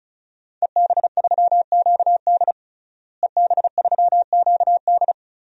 Morse code, heard as a keyed tone, about 36 wpm, 700 Hz. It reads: EB3QD EB3QD
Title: DXCC-100 Call Signs - 5 Characters - Repeated 2x - 35wpm